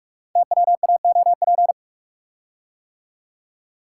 Morse code, heard as a keyed tone, about 44 wpm, 700 Hz. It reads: TWAOP
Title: Code Groups - 5 Letters - 45wpm